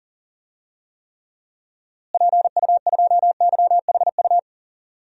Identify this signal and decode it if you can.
Morse code, heard as a keyed tone, about 40 wpm, 700 Hz. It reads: PU2YHU
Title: DXCC-100 Call Signs - 6 Characters - Repeated 2x - 40wpm